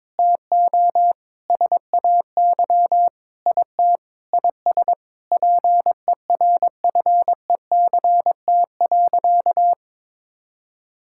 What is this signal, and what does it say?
Morse code, heard as a keyed tone, about 22 wpm, 700 Hz. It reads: TO SAY IT IS PERFECT.
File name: practice / 22wpm / Sentences with 8-Word Phrases - 22wpm